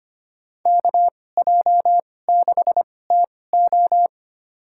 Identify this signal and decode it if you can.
Morse code, heard as a keyed tone, about 25 wpm, spaced as for 18 wpm, 700 Hz. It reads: KJ6TO
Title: USA Call Signs - 2x Character Spacing - 25wpm